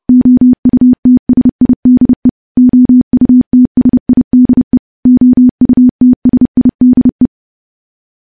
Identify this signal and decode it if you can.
Morse code, heard as a keyed tone, about 32 wpm, 250 Hz. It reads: OUTSIDE OUTSIDE OUTSIDE
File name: morse / morse-training-30WPM